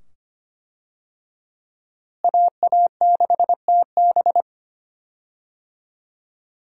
Morse code, heard as a keyed tone, about 25 wpm, 700 Hz. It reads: AA6TB
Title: USA Call Signs - 5 Characters - Repeated 2x - 25wpm